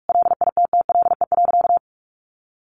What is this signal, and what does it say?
Morse code, heard as a keyed tone, about 45 wpm, 700 Hz. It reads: LITTLE.
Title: Sentences from Top 1500 Words 45wpm